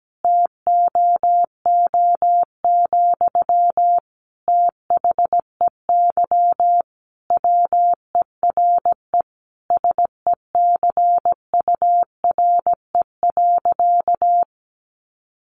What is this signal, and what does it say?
Morse code, heard as a keyed tone, about 17 wpm, 700 Hz. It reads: TOO, THEY WERE SECURE.
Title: Sentences from Top 1200 Words 17wpm